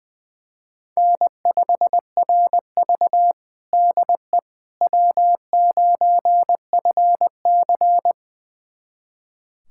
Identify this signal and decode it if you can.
Morse code, heard as a keyed tone, about 20 wpm, 700 Hz. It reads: N5RV DE W9FC
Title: USA Callsign DE Callsign - 4 Characters - 20wpm